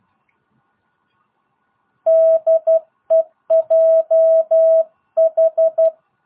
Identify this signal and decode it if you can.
Morse code, heard as a keyed tone, about 12 wpm, 660 Hz. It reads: DEJH